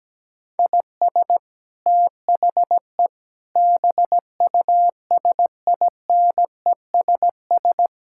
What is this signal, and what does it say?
Morse code, heard as a keyed tone, about 17 wpm, 700 Hz. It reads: IS THE BUSINESS
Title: Sentences from Top 1000 Words - Review Part 6 - 17wpm